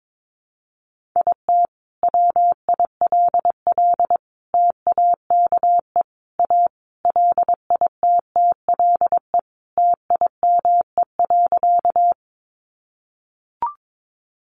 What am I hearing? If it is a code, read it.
Morse code, heard as a keyed tone, about 22 wpm, 700 Hz. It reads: IT WILL TAKE A LITTLE TIME.
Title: Sentences from Top 200 Words 22wpm